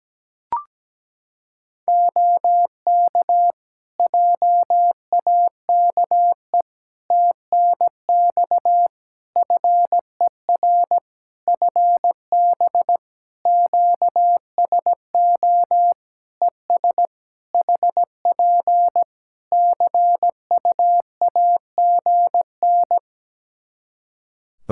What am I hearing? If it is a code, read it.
Morse code, heard as a keyed tone, about 17 wpm, 700 Hz. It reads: OK JAKE TNX FER FB QSO ES HP CUAGN
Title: Practice QSO - Exchange 4 - Ending - 17wpm